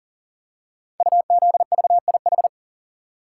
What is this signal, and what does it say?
Morse code, heard as a keyed tone, about 38 wpm, 700 Hz. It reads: UZVIH
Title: Code Groups - 5 Letters - 40wpm